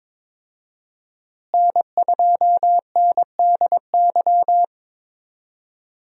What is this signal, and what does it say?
Morse code, heard as a keyed tone, about 22 wpm, 700 Hz. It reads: N2NDY
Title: USA Call Signs - 5 Characters - Repeated 2x - 22wpm